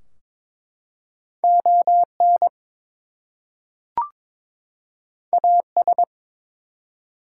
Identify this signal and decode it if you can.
Morse code, heard as a keyed tone, about 22 wpm, 700 Hz. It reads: ON AS
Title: Two Letter Words - 22wpm